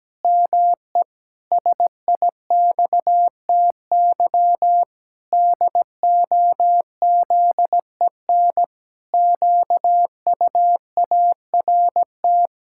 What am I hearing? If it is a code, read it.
Morse code, heard as a keyed tone, about 17 wpm, 700 Hz. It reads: ME SIXTY DOZEN QUART
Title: Pangram Challenge v2 - 17wpm